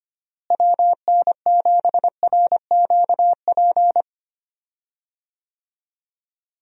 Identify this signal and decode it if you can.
Morse code, heard as a keyed tone, about 25 wpm, 700 Hz. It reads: WN7RQP
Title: USA Call Signs - 6 Characters - Repeated 2x - 25wpm